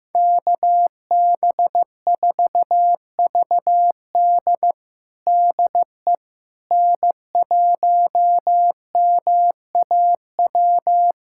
Morse code, heard as a keyed tone, about 15 wpm, 700 Hz. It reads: KB4VD DE N1MAW
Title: USA Callsign DE Callsign - 5 Characters - 15wpm